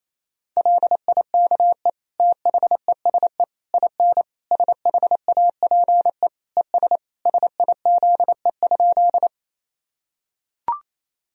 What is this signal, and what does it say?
Morse code, heard as a keyed tone, about 28 wpm, 700 Hz. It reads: LIKE THESE IN SHAPE ES SIZE?
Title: Sentences from Top 500 Words and Abbreviations 28wpm